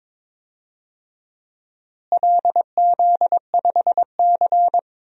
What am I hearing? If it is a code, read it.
Morse code, heard as a keyed tone, about 21 wpm, 700 Hz. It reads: LZ5C
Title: DXCC-100 Call Signs - 4 Characters - Repeated 3x - 22wpm